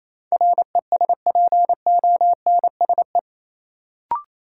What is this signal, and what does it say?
Morse code, heard as a keyed tone, about 28 wpm, 700 Hz. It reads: RESPONSE
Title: Sentences from Top 1400 Words - New Word Drill - 28wpm